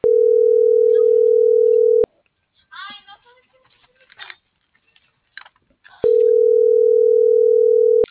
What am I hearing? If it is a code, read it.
Ringback tone (North America), 440 + 480 Hz.